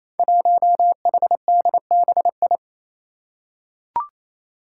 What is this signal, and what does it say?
Morse code, heard as a keyed tone, about 28 wpm, 700 Hz. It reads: 1HDBI
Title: Five Challenging Letter-Number Combinations 28wpm